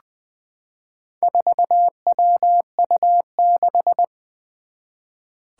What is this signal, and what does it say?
Morse code, heard as a keyed tone, about 20 wpm, 700 Hz. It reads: 4WU6
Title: Four Challenging Letter-Number Combinations 20wpm